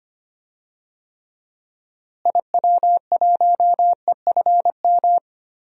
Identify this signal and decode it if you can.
Morse code, heard as a keyed tone, about 25 wpm, 700 Hz. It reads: IW1EFM
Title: DXCC-100 Call Signs - 6 Characters - Repeated 2x - 25wpm